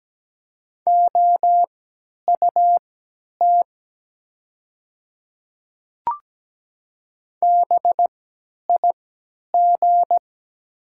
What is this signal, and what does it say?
Morse code, heard as a keyed tone, about 17 wpm, spaced as for 10 wpm, 700 Hz. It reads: OUT BIG
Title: 3x Character Spacing - Top 200 Words - 17wpm